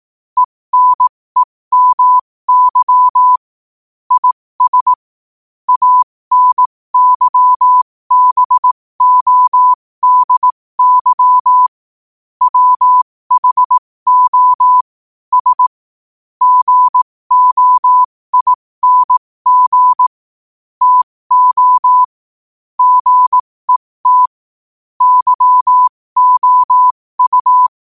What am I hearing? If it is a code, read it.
Morse code, heard as a keyed tone, about 18 wpm, spaced as for 15 wpm, 1000 Hz. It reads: ENEMY IS ANYBODY WHO S GOING TO GET YOU
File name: qotdfiles / 15wpm